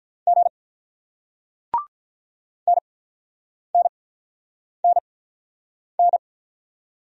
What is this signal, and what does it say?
Morse code, heard as a keyed tone, about 45 wpm, spaced as for 9 wpm, 700 Hz. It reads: G NNNN